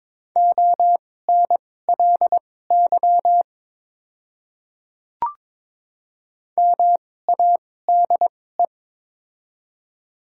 Morse code, heard as a keyed tone, about 21 wpm, spaced as for 16 wpm, 700 Hz. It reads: ONLY MADE
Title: 2x Character Spacing - Top 300 Words - 22wpm